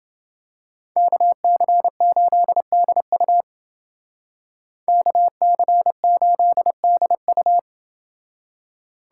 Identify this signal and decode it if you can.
Morse code, heard as a keyed tone, about 28 wpm, 700 Hz. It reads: KC8DU KC8DU